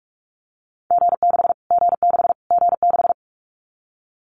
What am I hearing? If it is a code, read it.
Morse code, heard as a keyed tone, about 44 wpm, 700 Hz. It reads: G6 G6 G6
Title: Callsign Prefixes for Top 25 DXCC - Repeated 3x - 45wpm